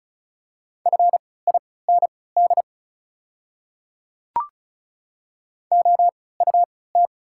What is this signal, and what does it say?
Morse code, heard as a keyed tone, about 36 wpm, spaced as for 20 wpm, 690 Hz. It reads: FIND OUT
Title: 3x Character Spacing - Top 100 Words - 35wpm